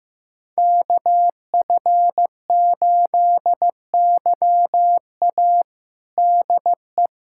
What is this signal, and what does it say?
Morse code, heard as a keyed tone, about 15 wpm, 700 Hz. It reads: KF8YA DE
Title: USA Callsign DE Callsign - 5 Characters - 15wpm